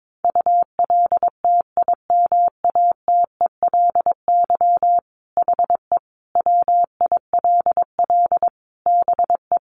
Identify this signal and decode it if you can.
Morse code, heard as a keyed tone, about 22 wpm, 700 Hz. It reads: ULTIMATELY HE WILL BE
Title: Sentences from Top 1200 Words 22wpm